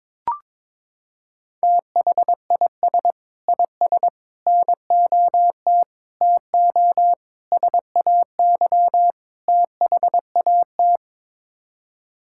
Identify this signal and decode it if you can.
Morse code, heard as a keyed tone, about 22 wpm, 700 Hz. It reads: THIS IS NOT TO SAY THAT
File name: practice / 22wpm / Sets of 6 Words - 22wpm